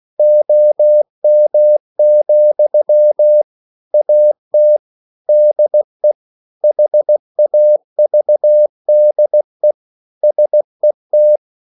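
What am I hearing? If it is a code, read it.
Morse code, heard as a keyed tone, about 16 wpm, 600 Hz. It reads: OM, AT DE HAVDE SET
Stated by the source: Slow (QRS) Morse code at 16 wpm, from CW med Gnister.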